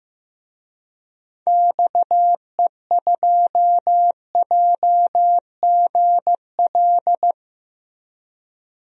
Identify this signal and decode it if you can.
Morse code, heard as a keyed tone, about 15 wpm, 700 Hz. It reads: XE2JGL
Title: DXCC-100 Call Signs - 6 Characters - Repeated 3x - 15wpm